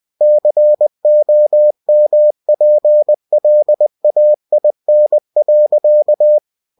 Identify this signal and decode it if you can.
Morse code, heard as a keyed tone, about 20 wpm, 600 Hz. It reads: COMPLAIN.
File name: cw-books / winnie-the-pooh / winnie-20wpm-page-0108